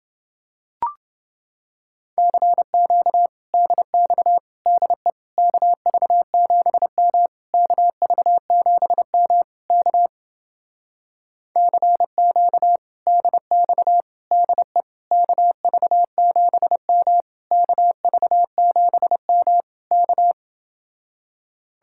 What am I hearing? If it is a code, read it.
Morse code, heard as a keyed tone, about 29 wpm, 700 Hz. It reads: CQ DX DE KV7M KV7M K CQ DX DE KV7M KV7M K